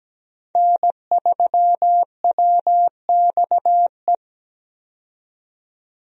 Morse code, heard as a keyed tone, about 17 wpm, 700 Hz. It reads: N3WXE
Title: USA Call Signs - 5 Characters - 17wpm